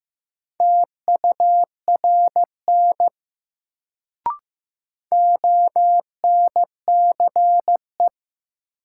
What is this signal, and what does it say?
Morse code, heard as a keyed tone, about 15 wpm, 700 Hz. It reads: TURN ONCE
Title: Sets of 4 Words - New Word Drill - 15wpm